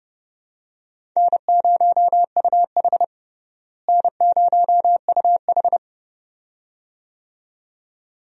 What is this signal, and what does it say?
Morse code, heard as a keyed tone, about 30 wpm, 700 Hz. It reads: N0UH N0UH
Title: USA Call Signs - 4 Characters - Repeated 2x - 30wpm